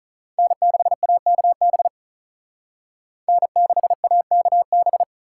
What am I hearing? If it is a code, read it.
Morse code, heard as a keyed tone, about 43 wpm, 700 Hz. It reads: N6AKB N6AKB
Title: Speed Racing - USA 1x3 Call Signs 35wpm